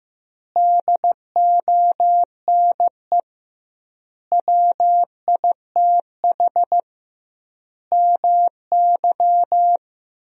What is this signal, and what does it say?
Morse code, heard as a keyed tone, about 16 wpm, 700 Hz. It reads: DONE WITH MY
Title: 2x Word-Spacing - Sentences from Top 500 Words - 15wpm